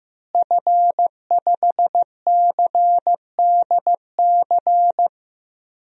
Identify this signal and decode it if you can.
Morse code, heard as a keyed tone, about 15 wpm, 700 Hz. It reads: F5CDC
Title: DXCC-100 Call Signs - 5 Characters - Repeated 2x - 15wpm